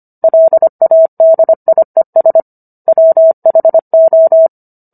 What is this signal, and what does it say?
Morse code, heard as a keyed tone, about 25 wpm, 650 Hz. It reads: LADIES WHO